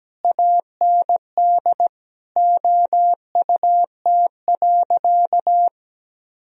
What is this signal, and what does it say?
Morse code, heard as a keyed tone, about 17 wpm, 700 Hz. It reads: AND OUT.
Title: Sentences from Top 1000 Words - Review Part 2 - 17wpm